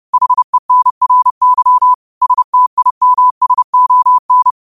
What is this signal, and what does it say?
Morse code, heard as a keyed tone, about 30 wpm, 1000 Hz. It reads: HENRY STIMSON